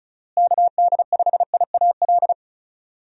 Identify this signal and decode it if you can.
Morse code, heard as a keyed tone, about 34 wpm, 700 Hz. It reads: KD5IAL
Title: USA Call Signs - 6 Characters - Repeated 2x - 35wpm